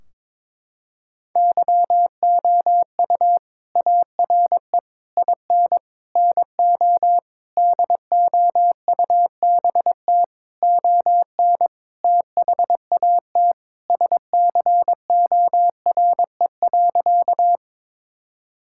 Morse code, heard as a keyed tone, about 22 wpm, 700 Hz. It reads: YOU ARE IN NO DOUBT ON THAT SCORE.
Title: Sentences from Top 1600 Words 22wpm